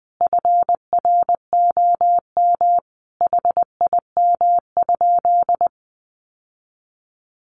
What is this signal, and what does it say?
Morse code, heard as a keyed tone, about 19 wpm, 700 Hz. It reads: FROM HIM?
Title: Sentences from Top 1300 Words 20wpm